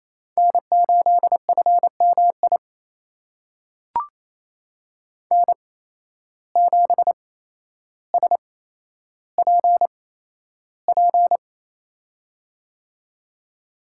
Morse code, heard as a keyed tone, about 28 wpm, spaced as for 12 wpm, 700 Hz. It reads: N8FMI N7SPP
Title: USA Call Signs - 5 Characters - 8x Character Spacing - 28wpm